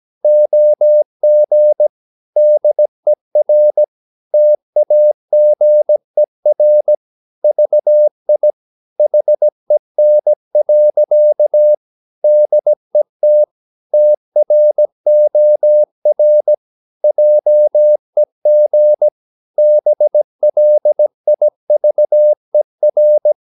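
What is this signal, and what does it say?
Morse code, heard as a keyed tone, about 17 wpm, 600 Hz. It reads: OG DER TAGER VI HEN. DET TROR JEG BLIVER
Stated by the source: Slow (QRS) Morse practice at 17 wpm, from CW med Gnister.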